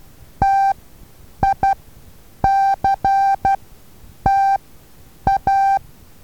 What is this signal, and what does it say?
Morse code, heard as a keyed tone, about 12 wpm, spaced as for 8 wpm, 780 Hz. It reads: TICTA
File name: Taviro HA6VA A